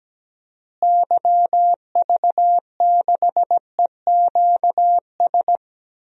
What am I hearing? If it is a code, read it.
Morse code, heard as a keyed tone, about 17 wpm, 700 Hz. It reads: YV6EQS
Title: DXCC-100 Call Signs - 6 Characters - Repeated 3x - 17wpm